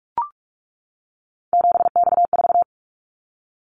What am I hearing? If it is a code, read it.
Morse code, heard as a keyed tone, about 44 wpm, 700 Hz. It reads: 7X4
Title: Callsign Prefixes for Top 100 DXCC - 45wpm